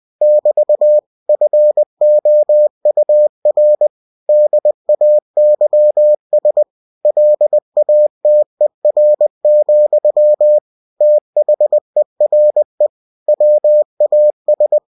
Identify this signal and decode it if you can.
Morse code, heard as a keyed tone, about 20 wpm, 600 Hz. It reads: <BT> FOUR DAYS LATER, THERE WAS
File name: cw-books / winnie-the-pooh / winnie-20wpm-page-0094